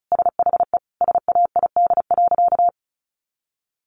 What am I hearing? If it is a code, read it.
Morse code, heard as a keyed tone, about 34 wpm, 700 Hz. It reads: SHE SAID.
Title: Sentences from Top 1100 Words 35wpm